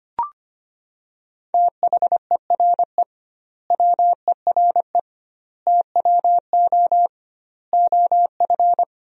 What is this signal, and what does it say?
Morse code, heard as a keyed tone, about 25 wpm, 700 Hz. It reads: THERE WERE TWO OF
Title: 2x Word-Spacing - Sentences from Top 100 Words - 25wpm